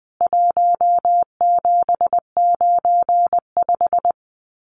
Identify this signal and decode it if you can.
Morse code, heard as a keyed tone, about 20 wpm, 700 Hz. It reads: 1795
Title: CWT Exchange 20wpm